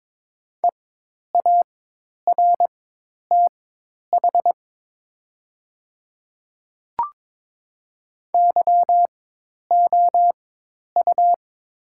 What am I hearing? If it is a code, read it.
Morse code, heard as a keyed tone, about 22 wpm, spaced as for 10 wpm, 700 Hz. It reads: EARTH YOU
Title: 4x Character Spacing - Top 200 Words - 22wpm